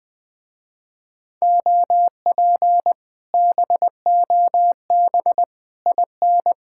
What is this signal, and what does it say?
Morse code, heard as a keyed tone, about 20 wpm, 700 Hz. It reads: OP BOB IN
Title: Speed-Racing Common QSO Phrases - 17wpm